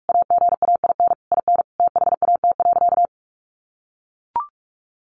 Morse code, heard as a keyed tone, about 44 wpm, 700 Hz. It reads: AGAIN IN THAT.
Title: Sentences from Top 200 Words 45wpm